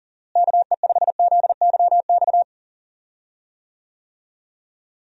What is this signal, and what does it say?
Morse code, heard as a keyed tone, about 40 wpm, 700 Hz. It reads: KE5ZYX
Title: USA Call Signs - 6 Characters - Repeated 2x - 40wpm